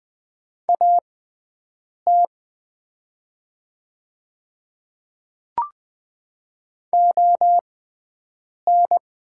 Morse code, heard as a keyed tone, about 20 wpm, spaced as for 7 wpm, 700 Hz. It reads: AT ON
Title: 6x Character Spacing - Top 100 Words - 20wpm